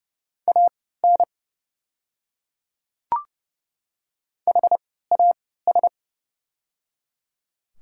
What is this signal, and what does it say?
Morse code, heard as a keyed tone, about 30 wpm, spaced as for 17 wpm, 700 Hz. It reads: AN HAS
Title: 3x Character Spacing - Top 200 Words - 30wpm